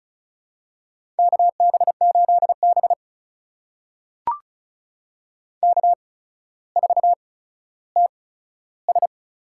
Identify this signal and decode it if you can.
Morse code, heard as a keyed tone, about 36 wpm, spaced as for 15 wpm, 700 Hz. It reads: KB8B K4TS
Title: USA Call Signs - 8x Character Spacing - 35wpm